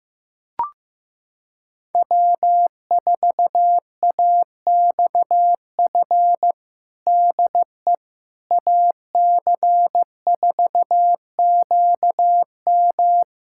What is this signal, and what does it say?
Morse code, heard as a keyed tone, about 15 wpm, 700 Hz. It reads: W4AXF DE AC4QM
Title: USA Callsign DE Callsign - 5 Characters - 15wpm